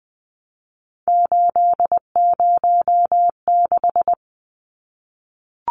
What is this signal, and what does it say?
Morse code, heard as a keyed tone, about 20 wpm, 700 Hz. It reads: 806
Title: Numbers - 3-Digit 20wpm